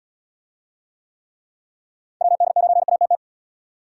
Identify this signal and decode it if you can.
Morse code, heard as a keyed tone, about 78 wpm, 700 Hz. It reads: WD9ATT